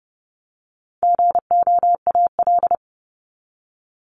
Morse code, heard as a keyed tone, about 28 wpm, 700 Hz. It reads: GOAL
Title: Top 751-1000 Words 30wpm